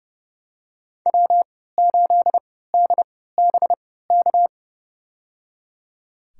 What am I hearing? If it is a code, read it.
Morse code, heard as a keyed tone, about 30 wpm, spaced as for 17 wpm, 700 Hz. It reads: W8DBK
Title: USA Call Signs - 5 Characters - 3x Character Spacing - 30wpm